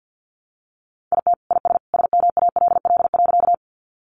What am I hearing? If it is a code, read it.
Morse code, heard as a keyed tone, about 48 wpm, 700 Hz. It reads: IT IS SMALL.
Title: Easy Phrases 50wpm